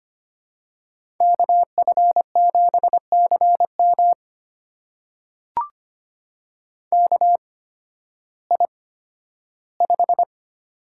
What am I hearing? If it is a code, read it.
Morse code, heard as a keyed tone, about 25 wpm, spaced as for 13 wpm, 700 Hz. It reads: KF7CM KI5
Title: USA Call Signs - 8x Character Spacing - 25wpm